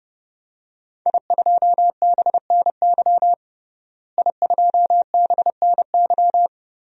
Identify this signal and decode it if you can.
Morse code, heard as a keyed tone, about 30 wpm, 700 Hz. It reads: I2BNY I2BNY
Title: DXCC-100 Call Signs - 5 Characters - Repeated 3x - 30wpm